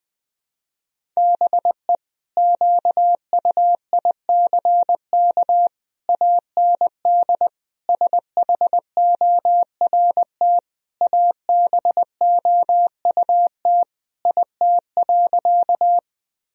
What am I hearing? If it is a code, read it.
Morse code, heard as a keyed tone, about 21 wpm, 700 Hz. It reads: BE QUICK AND SHORT ABOUT IT.
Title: Sentences from Top 1000 Words - Review Part 2 - 20wpm